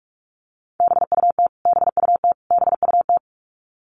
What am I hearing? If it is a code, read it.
Morse code, heard as a keyed tone, about 44 wpm, 700 Hz. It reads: BUT BUT BUT
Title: Top 500 Words Repeated 45wpm